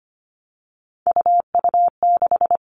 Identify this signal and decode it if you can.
Morse code, heard as a keyed tone, about 25 wpm, 700 Hz. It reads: UU6
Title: All Cut Numbers - 3-Digit 25wpm